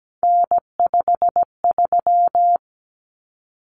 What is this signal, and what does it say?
Morse code, heard as a keyed tone, about 17 wpm, 700 Hz. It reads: N53
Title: Common Cut Numbers - 3-Digit 17wpm